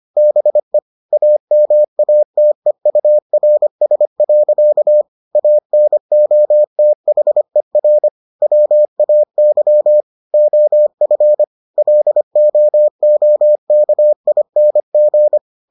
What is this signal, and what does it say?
Morse code, heard as a keyed tone, about 25 wpm, 600 Hz. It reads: BE AMATEURS. ANOTHER WAY OF LOOKING